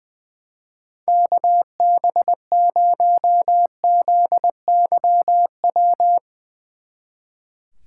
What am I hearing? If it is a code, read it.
Morse code, heard as a keyed tone, about 22 wpm, 700 Hz. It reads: KB0ZYW
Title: Speed Racing - USA 2x3 Call Signs 20wpm